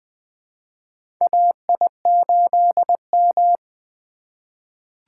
Morse code, heard as a keyed tone, about 20 wpm, 700 Hz. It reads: AI8M
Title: USA Call Signs - 4 Characters - 2x Character Spacing - 20wpm